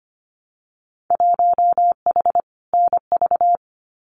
Morse code, heard as a keyed tone, about 25 wpm, 700 Hz. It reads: 1H NV
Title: Winter Field Day Exchanges 25wpm